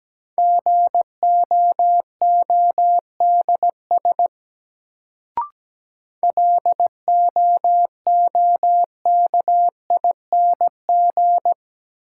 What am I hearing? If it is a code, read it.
Morse code, heard as a keyed tone, about 17 wpm, 700 Hz. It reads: GOODS LOOKING
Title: Sentences from Top 1000 Words - Review Part 6 - New Word Drill - 17wpm